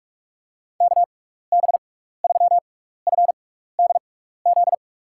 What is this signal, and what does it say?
Morse code, heard as a keyed tone, about 46 wpm, spaced as for 16 wpm, 700 Hz. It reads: KB3FDZ
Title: USA Call Signs - 6 Characters - 6x Character Spacing - 45wpm